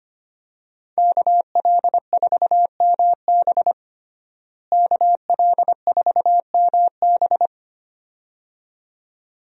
Morse code, heard as a keyed tone, about 25 wpm, 700 Hz. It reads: KL4MB KL4MB
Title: USA Call Signs - 5 Characters - Repeated 2x - 25wpm